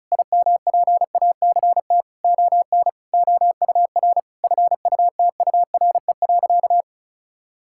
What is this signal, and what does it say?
Morse code, heard as a keyed tone, about 36 wpm, 700 Hz. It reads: IMPACT ON OUR FUTURE.
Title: Sentences from Top 1000 Words - Review Part 6 - 35wpm